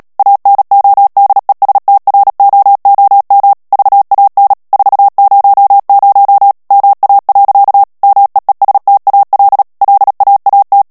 Morse code, heard as a keyed tone, about 37 wpm, 800 Hz. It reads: ANODESTROOM VAN 400 MA. MEESTAL LAAT
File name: morse mp3 / HOLLANDAIS texte clair 2 vitesse 35 mots minute